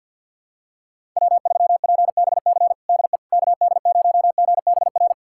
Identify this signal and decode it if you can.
Morse code, heard as a keyed tone, about 53 wpm, 700 Hz. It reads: W3PBC DE KD0GBR
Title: USA Callsign DE Callsign 50wpm